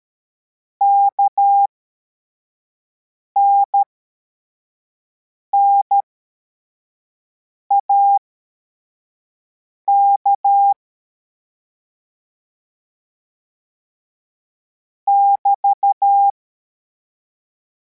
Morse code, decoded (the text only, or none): KNNAK <BT>